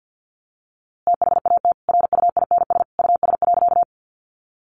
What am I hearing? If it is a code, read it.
Morse code, heard as a keyed tone, about 50 wpm, 700 Hz. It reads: THAT RUINS US.
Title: A Sound Of Thunder 50wpm